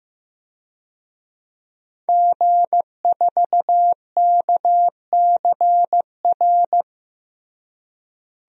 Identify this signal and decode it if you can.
Morse code, heard as a keyed tone, about 15 wpm, 700 Hz. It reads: G4KCR
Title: DXCC-100 Call Signs - 5 Characters - Repeated 2x - 15wpm